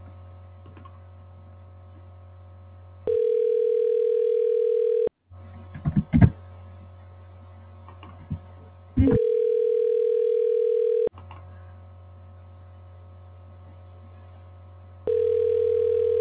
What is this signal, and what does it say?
Ringback tone (North America), 440 + 480 Hz.